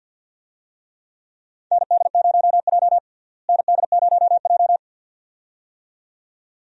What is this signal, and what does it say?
Morse code, heard as a keyed tone, about 53 wpm, 700 Hz. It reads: ND0J ND0J
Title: USA Call Signs - 4 Characters - Repeated 2x - 50wpm